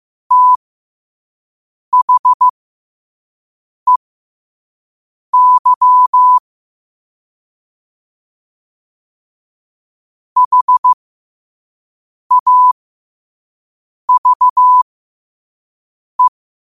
Morse code, heard as a keyed tone, about 15 wpm, spaced as for 5 wpm, 1000 Hz. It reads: THEY HAVE
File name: qotdfiles / 5wpm